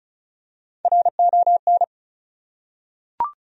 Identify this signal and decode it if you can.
Morse code, heard as a keyed tone, about 35 wpm, 700 Hz. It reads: RON
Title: USA Common Names - Max 4 Letters - 35wpm